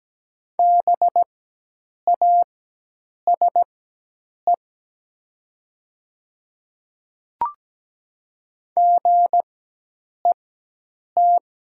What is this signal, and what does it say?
Morse code, heard as a keyed tone, about 17 wpm, spaced as for 8 wpm, 700 Hz. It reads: BASE GET
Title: 4x Character Spacing - Top 300 Words - 17wpm